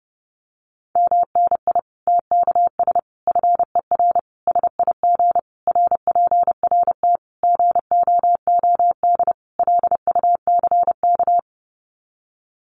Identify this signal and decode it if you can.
Morse code, heard as a keyed tone, about 30 wpm, 700 Hz. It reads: MNI TKS FER SIG RPRT GOOD LUCK
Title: Common QSO Phrases 30wpm